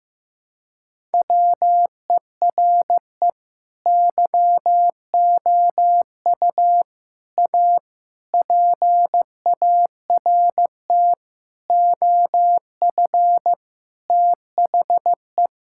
Morse code, heard as a keyed tone, about 15 wpm, 700 Hz. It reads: WERE YOU A PART OF THE
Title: Sentences from Top 1000 Words - Review Part 6 - 15wpm